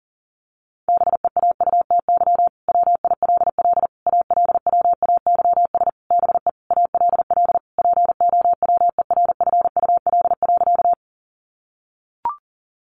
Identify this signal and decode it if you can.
Morse code, heard as a keyed tone, about 40 wpm, 700 Hz. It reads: BEAUTY WILL ALWAYS BE ALL POWERFUL.
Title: Sentences from Top 1300 Words 40wpm